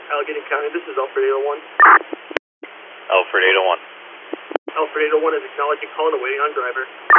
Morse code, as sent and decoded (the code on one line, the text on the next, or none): none